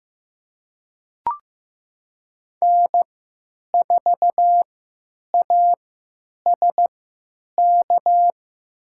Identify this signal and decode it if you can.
Morse code, heard as a keyed tone, about 15 wpm, spaced as for 9 wpm, 700 Hz. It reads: N4ASK
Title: USA Call Signs - 3x Character Spacing - 15wpm